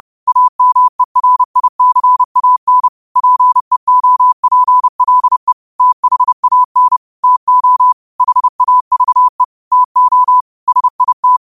Morse code, heard as a keyed tone, about 30 wpm, 1000 Hz. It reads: AMERICAN PEOPLE THAN TO HAVE TO SIT
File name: 30wpm